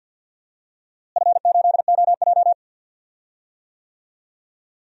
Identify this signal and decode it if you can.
Morse code, heard as a keyed tone, about 47 wpm, 700 Hz. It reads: F8OJ
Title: DXCC-100 Call Signs - 4 Characters - Repeated 3x - 50wpm